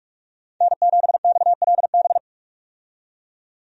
Morse code, heard as a keyed tone, about 46 wpm, 700 Hz. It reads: N7XLB
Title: USA Call Signs - 5 Characters - 45wpm